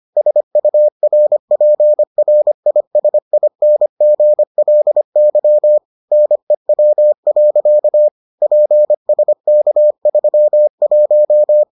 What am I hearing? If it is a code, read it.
Morse code, heard as a keyed tone, about 25 wpm, 600 Hz. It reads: SURPRISINGLY NEW. PSK31